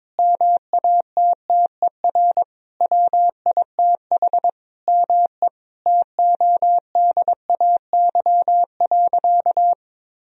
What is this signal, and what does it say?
Morse code, heard as a keyed tone, about 22 wpm, 700 Hz. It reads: MATTER WITH ME TODAY.
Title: Sentences from Top 1000 Words - Review Part 4 - 22wpm